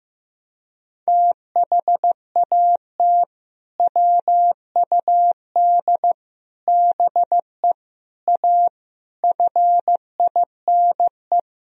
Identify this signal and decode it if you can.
Morse code, heard as a keyed tone, about 15 wpm, 700 Hz. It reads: THAT WUD BE A FINE
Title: Sentences from Top 500 Words and Abbreviations 15wpm